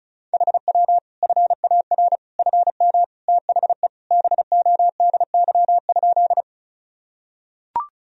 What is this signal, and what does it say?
Morse code, heard as a keyed tone, about 34 wpm, 700 Hz. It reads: HW FAR FM THE BODY?
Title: Sentences from Top 400 Words and Abbreviations 35wpm